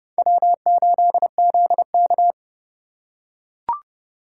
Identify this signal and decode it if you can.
Morse code, heard as a keyed tone, about 29 wpm, 700 Hz. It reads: W8ZK